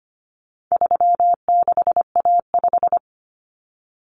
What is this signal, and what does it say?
Morse code, heard as a keyed tone, about 25 wpm, 700 Hz. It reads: 36A5
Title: Common Cut Numbers - 4-Digit 25wpm